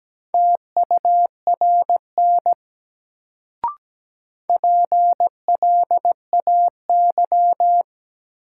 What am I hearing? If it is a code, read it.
Morse code, heard as a keyed tone, about 17 wpm, 700 Hz. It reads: TURN PLAY
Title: Sentences from Top 200 Words - New Word Drill - 17wpm